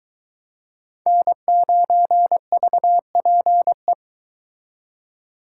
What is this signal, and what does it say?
Morse code, heard as a keyed tone, about 22 wpm, 700 Hz. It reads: N9VPE